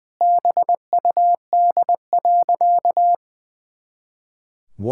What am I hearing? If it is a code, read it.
Morse code, heard as a keyed tone, about 21 wpm, 700 Hz. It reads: BUD.
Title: Pangram Challenge v2 - 20wpm